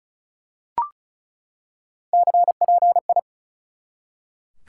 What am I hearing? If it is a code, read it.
Morse code, heard as a keyed tone, about 34 wpm, 700 Hz. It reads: CPI
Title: Top 180 QSO Elements 35wpm